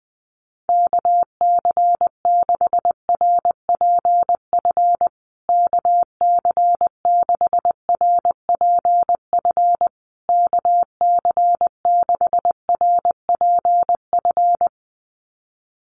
Morse code, heard as a keyed tone, about 20 wpm, 700 Hz. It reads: KC6RPF KC6RPF KC6RPF
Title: USA Call Signs Repeated 3x 20wpm